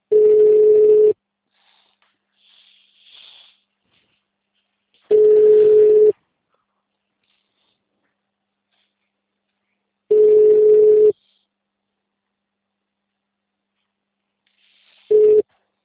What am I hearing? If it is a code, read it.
Ringback tone (Europe, 425 Hz).
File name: out- -2016- - - .38